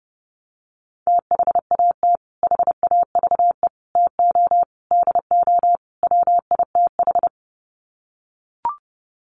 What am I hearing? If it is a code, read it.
Morse code, heard as a keyed tone, about 30 wpm, 700 Hz. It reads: THAT HAVE TO DO WITH
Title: Sets of 5 Words Encore2 30wpm